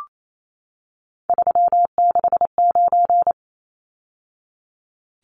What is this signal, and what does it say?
Morse code, heard as a keyed tone, about 28 wpm, 700 Hz. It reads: 369
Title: Numbers - 3-Digit 28wpm